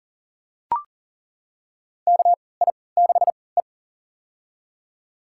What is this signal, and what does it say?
Morse code, heard as a keyed tone, about 40 wpm, spaced as for 23 wpm, 700 Hz. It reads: KI6E
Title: USA Call Signs - 4 Characters - 3x Character Spacing - 40wpm